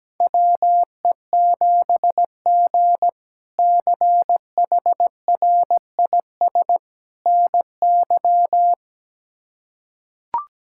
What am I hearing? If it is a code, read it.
Morse code, heard as a keyed tone, about 17 wpm, 700 Hz. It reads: WE7G CHRIS NY
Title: K1USN SST QSO Practice - 17wpm